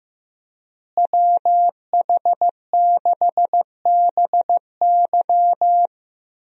Morse code, heard as a keyed tone, about 16 wpm, 700 Hz. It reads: WH6BY
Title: USA Call Signs - 5 Characters - Repeated 2x - 15wpm